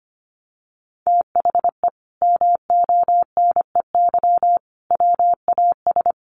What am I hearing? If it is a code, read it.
Morse code, heard as a keyed tone, about 25 wpm, 700 Hz. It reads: THE MONEY WAS
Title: Sentences from Top 1400 Words 25wpm